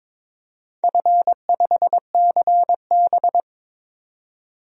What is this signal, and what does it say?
Morse code, heard as a keyed tone, about 22 wpm, 700 Hz. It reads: F5CB
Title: DXCC-100 Call Signs - 4 Characters - 22wpm